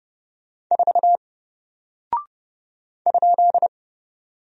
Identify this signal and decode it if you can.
Morse code, heard as a keyed tone, about 30 wpm, 700 Hz. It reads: 4 ?
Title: Single Character (Letter:Number:Punctuation:Prosign) 30wpm